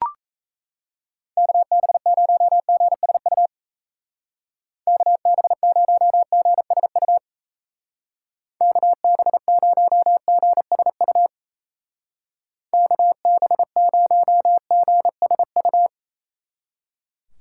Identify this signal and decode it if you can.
Morse code, heard as a keyed tone, about 34 wpm, 700 Hz. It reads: KB0GSU KB0GSU KB0GSU KB0GSU
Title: Speed Racing - USA 2x3 Call Signs 28wpm